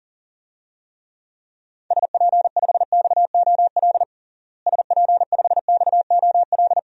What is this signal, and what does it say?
Morse code, heard as a keyed tone, about 40 wpm, 700 Hz. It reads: SP5XOL SP5XOL
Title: DXCC-100 Call Signs - 6 Characters - Repeated 3x - 40wpm